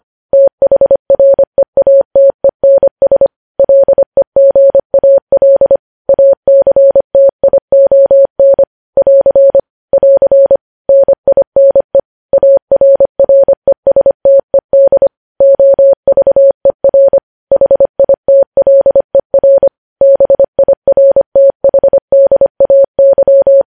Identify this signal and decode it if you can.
Morse code, heard as a keyed tone, about 25 wpm, 550 Hz. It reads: THREATENS LEGAL ACTION <AR> <AR> NINE ARRESTED OVER HITLER BIRTHDAY
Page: news headlines in morse code at 25 wpm